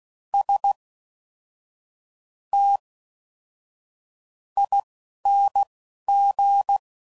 Morse code, heard as a keyed tone, about 16 wpm, spaced as for 6 wpm, 800 Hz. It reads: STING